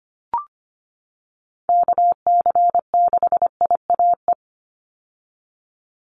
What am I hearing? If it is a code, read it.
Morse code, heard as a keyed tone, about 25 wpm, 700 Hz. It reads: KC6IAE
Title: USA Call Signs 25wpm